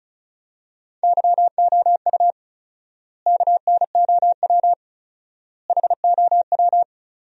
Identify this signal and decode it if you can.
Morse code, heard as a keyed tone, about 36 wpm, 700 Hz. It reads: YOU KNOW HOW
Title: 4x Word-Spacing - Sentences from Top 100 Words - 35wpm